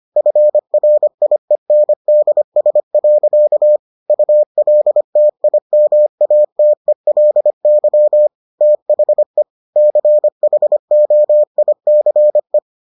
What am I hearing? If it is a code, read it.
Morse code, heard as a keyed tone, about 25 wpm, 600 Hz. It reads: FRIENDS. ULTIMATELY THE CHOICE